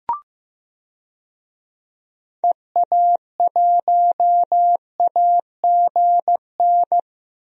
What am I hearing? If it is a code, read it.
Morse code, heard as a keyed tone, about 15 wpm, 700 Hz. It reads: EA1AGN
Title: DXCC-100 Call Signs - 6 Characters - Repeated 3x - 15wpm